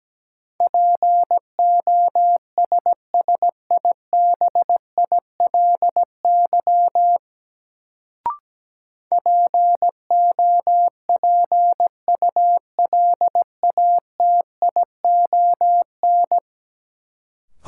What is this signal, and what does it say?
Morse code, heard as a keyed tone, about 17 wpm, 700 Hz. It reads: POSSIBILY POPULATION
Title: Sentences from Top 1300 Words - New Word Drill Alt - 17wpm